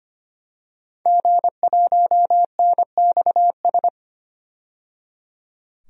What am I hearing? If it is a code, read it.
Morse code, heard as a keyed tone, about 25 wpm, 700 Hz. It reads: G1NXS
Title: DXCC-100 Call Signs - 5 Characters - 25wpm